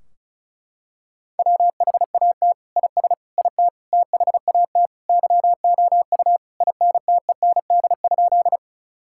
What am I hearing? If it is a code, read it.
Morse code, heard as a keyed tone, about 34 wpm, 700 Hz. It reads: WHAT IS IT THAT YOU INTEND?
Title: Sentences from Top 1600 Words 35wpm